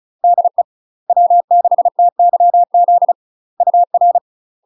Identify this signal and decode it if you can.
Morse code, heard as a keyed tone, about 34 wpm, 700 Hz. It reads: DE W6TYZ UR
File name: UK MASTER QSOs TRAINING / 35WPM / UK MASTER QSOs 0040